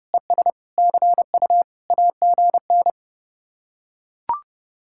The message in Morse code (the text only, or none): ES CU AGN